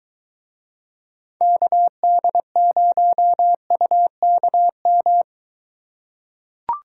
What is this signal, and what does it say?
Morse code, heard as a keyed tone, about 23 wpm, 700 Hz. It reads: KD0UKM